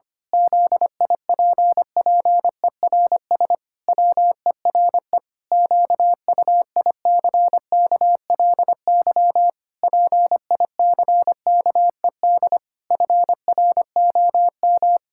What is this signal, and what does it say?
Morse code, heard as a keyed tone, about 25 wpm, 700 Hz. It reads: ZIPPERS WERE QUICKLY PICKED FROM
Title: Pangram Challenge v2 - 25wpm